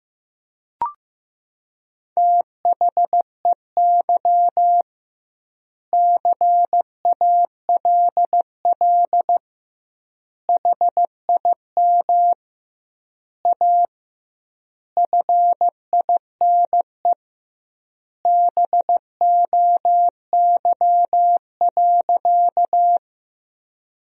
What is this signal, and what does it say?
Morse code, heard as a keyed tone, about 15 wpm, 700 Hz. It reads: THEY CALL HIM A FINE BOY.
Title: 2x Word-Spacing - Sentences from Top 500 Words - 15wpm